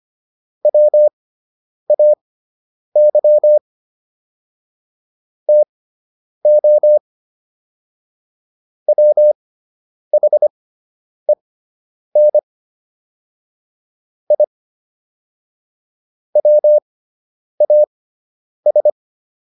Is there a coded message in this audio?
Yes: Morse code from 0.7 to 18.9 s, heard as a keyed tone, about 25 wpm, spaced as for 9 wpm, 600 Hz. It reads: WAY TO WHEN I WAS